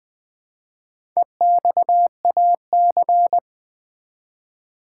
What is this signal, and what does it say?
Morse code, heard as a keyed tone, about 20 wpm, 700 Hz. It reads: EXAC
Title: Build A Word - 1 Letter At A Time - 5-Letter Words - 20wpm